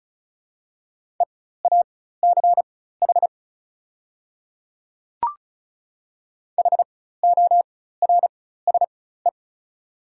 Morse code, heard as a keyed tone, about 36 wpm, spaced as for 16 wpm, 690 Hz. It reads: EACH HORSE
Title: 4x Character Spacing - Top 300 Words - 35wpm